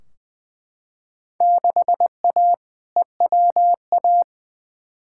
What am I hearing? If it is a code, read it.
Morse code, heard as a keyed tone, about 20 wpm, 700 Hz. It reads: 6A EWA
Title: Field Day Exchanges Encore - 20wpm